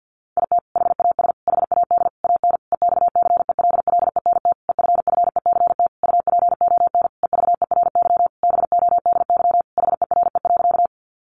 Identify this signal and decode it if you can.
Morse code, heard as a keyed tone, about 50 wpm, 700 Hz. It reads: IT HAS HAD AN EXCELLENT EFFECT UPON EVERY BODY HERE.
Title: Sentences from Top 1100 Words 50wpm